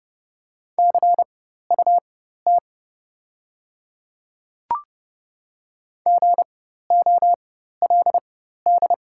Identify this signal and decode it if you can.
Morse code, heard as a keyed tone, about 30 wpm, spaced as for 14 wpm, 700 Hz. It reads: CUT GOLD
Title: 4x Character Spacing - Top 500 Words - 30wpm